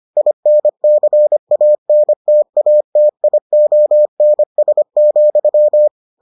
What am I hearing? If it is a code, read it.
Morse code, heard as a keyed tone, about 25 wpm, 600 Hz. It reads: INCANTATIONS,